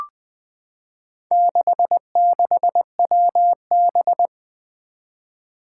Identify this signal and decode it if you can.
Morse code, heard as a keyed tone, about 20 wpm, 700 Hz. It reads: 66WB
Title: Four Challenging Letter-Number Combinations 20wpm